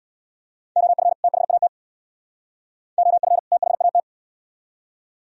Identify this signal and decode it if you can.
Morse code, heard as a keyed tone, about 75 wpm, 700 Hz. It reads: OF THAT OF THAT